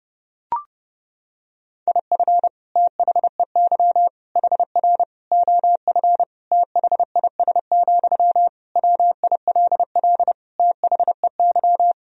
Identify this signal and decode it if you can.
Morse code, heard as a keyed tone, about 30 wpm, 700 Hz. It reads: IF THEY HR OF THIS, WILL THEY
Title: Sentences from Top 400 Words and Abbreviations 30wpm